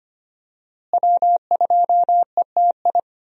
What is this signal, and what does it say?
Morse code, heard as a keyed tone, about 25 wpm, 700 Hz. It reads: W2ETI
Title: USA Call Signs - 5 Characters - Repeated 2x - 25wpm